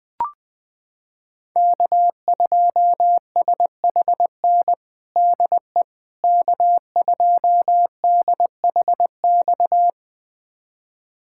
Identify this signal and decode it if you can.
Morse code, heard as a keyed tone, about 20 wpm, 700 Hz. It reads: K2SHN DE K2DHX
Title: USA Callsign DE Callsign - 5 Characters - 20wpm